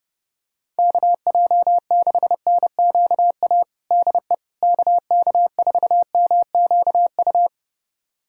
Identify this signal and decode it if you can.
Morse code, heard as a keyed tone, about 30 wpm, 700 Hz. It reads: KJ6NQA DE KK4MQU
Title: USA Callsign DE Callsign 30wpm